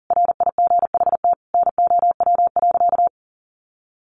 Morse code, heard as a keyed tone, about 37 wpm, 700 Hz. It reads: RIGHT NOW.
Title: Sentences from Top 700 Words 40wpm